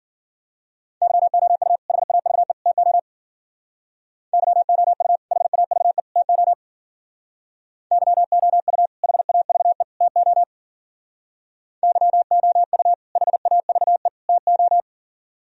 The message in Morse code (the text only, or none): YOU HAVE TO YOU HAVE TO YOU HAVE TO YOU HAVE TO